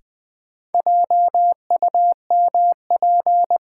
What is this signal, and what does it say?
Morse code, heard as a keyed tone, about 20 wpm, 700 Hz. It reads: JUMP
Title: Sentences from Top 1300 Words - New Word Drill - 20wpm